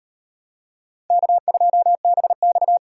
Morse code, heard as a keyed tone, about 38 wpm, 700 Hz. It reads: K2BX